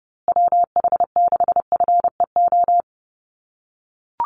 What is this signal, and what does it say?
Morse code, heard as a keyed tone, about 30 wpm, 700 Hz. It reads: WH6FEO
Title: USA Call Signs Repeated 3x 30wpm